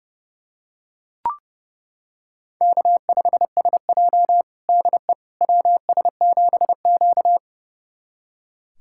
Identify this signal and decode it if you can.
Morse code, heard as a keyed tone, about 30 wpm, 700 Hz. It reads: K5SJ DE WS7Q
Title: USA Callsign DE Callsign - 4 Characters - 30wpm